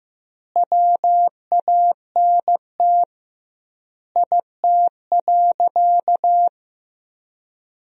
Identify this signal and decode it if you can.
Morse code, heard as a keyed tone, about 15 wpm, 700 Hz. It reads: WANT IT.
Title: 2x Word-Spacing - Sentences from Top 300 Words - 15wpm